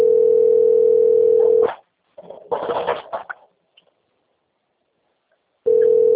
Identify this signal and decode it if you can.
Ringback tone (North America), 440 + 480 Hz.
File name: out- - - - - .26939